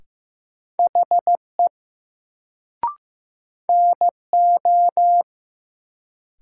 Morse code, heard as a keyed tone, about 15 wpm, 700 Hz. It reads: HE NO
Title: Sentences from Top 100 Words - New Word Drill - 15wpm